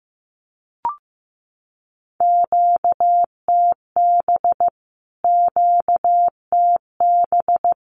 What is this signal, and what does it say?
Morse code, heard as a keyed tone, about 15 wpm, 700 Hz. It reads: QTB QTB
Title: Q-Codes Repeated 3x 15wpm